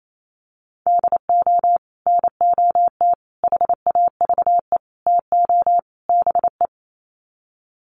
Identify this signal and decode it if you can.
Morse code, heard as a keyed tone, about 28 wpm, 700 Hz. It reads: DO NOT HAVE TO BE
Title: Sets of 5 Words 28wpm